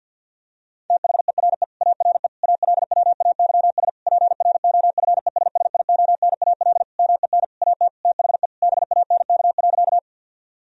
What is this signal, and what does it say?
Morse code, heard as a keyed tone, about 50 wpm, 700 Hz. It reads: THERE ARE ALWAYS PROFESSIONAL MEN AT THE BANK.
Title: Sentences from Top 1000 Words - Review Part 4 - 50wpm